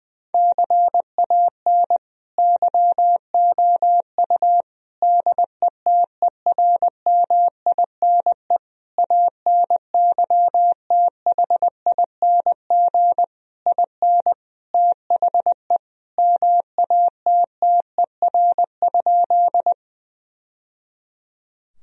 Morse code, heard as a keyed tone, about 21 wpm, 700 Hz. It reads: CAN YOU DETERMINE ANYTHING IN THE MATTER?
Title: Sentences from Top 1000 Words - Review Part 4 - 20wpm